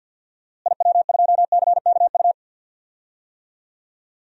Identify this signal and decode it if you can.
Morse code, heard as a keyed tone, about 52 wpm, 700 Hz. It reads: IW2CKU
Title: DXCC-100 Call Signs - 6 Characters - Repeated 2x - 50wpm